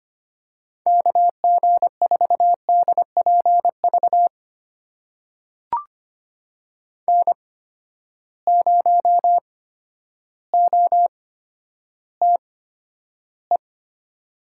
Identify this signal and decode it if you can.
Morse code, heard as a keyed tone, about 25 wpm, spaced as for 11 wpm, 700 Hz. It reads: KG4DPV N0OTE
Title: USA Call Signs - 8x Character Spacing - 25wpm